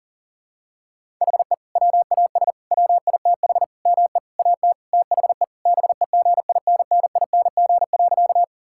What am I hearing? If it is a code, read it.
Morse code, heard as a keyed tone, about 40 wpm, 700 Hz. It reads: HE WAS WITH ME AT THE BEGINNING.
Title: Sentences from Top 1000 Words - Review Part 5 - 40wpm